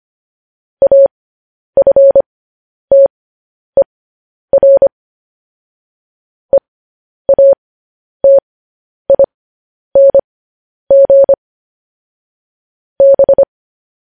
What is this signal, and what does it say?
Morse code, heard as a keyed tone, about 25 wpm, spaced as for 10 wpm, 550 Hz. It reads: AFTER EATING B
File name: podcast / News headlines-1x-25x10